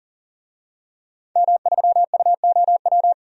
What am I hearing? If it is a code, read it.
Morse code, heard as a keyed tone, about 40 wpm, 700 Hz. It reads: M3UOW
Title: DXCC-100 Call Signs - 5 Characters - 40wpm